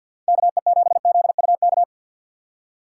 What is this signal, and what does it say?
Morse code, heard as a keyed tone, about 53 wpm, 700 Hz. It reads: KE7ZUK
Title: USA Call Signs - 6 Characters - 6x Character Spacing - 50wpm